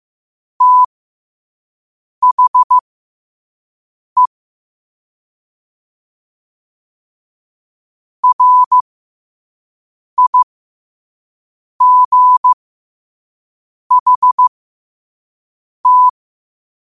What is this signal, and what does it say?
Morse code, heard as a keyed tone, about 15 wpm, spaced as for 5 wpm, 1000 Hz. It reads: THE RIGHT
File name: qotdfiles / 5wpm